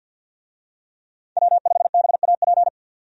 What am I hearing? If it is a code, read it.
Morse code, heard as a keyed tone, about 50 wpm, 700 Hz. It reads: W5BAP
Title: USA Call Signs - 5 Characters - 50wpm